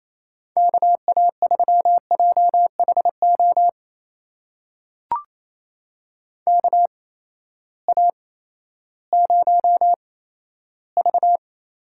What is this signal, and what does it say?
Morse code, heard as a keyed tone, about 28 wpm, spaced as for 14 wpm, 700 Hz. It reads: KA3JHO KA0V
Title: USA Call Signs - 6 Characters - 8x Character Spacing - 28wpm